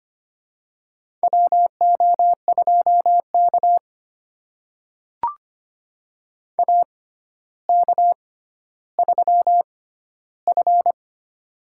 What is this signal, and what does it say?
Morse code, heard as a keyed tone, about 25 wpm, spaced as for 13 wpm, 700 Hz. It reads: WO2K AK3F
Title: USA Call Signs - 4 Characters - 6x Character Spacing - 25wpm